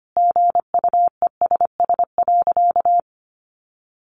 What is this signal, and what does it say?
Morse code, heard as a keyed tone, about 25 wpm, 700 Hz. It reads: GUESS.
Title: Sentences from Top 1500 Words 25wpm